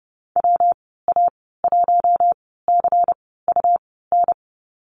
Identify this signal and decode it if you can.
Morse code, heard as a keyed tone, about 30 wpm, spaced as for 17 wpm, 700 Hz. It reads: WA1CUN
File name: Lesson 19g - Letter w - Call Signs 30wpm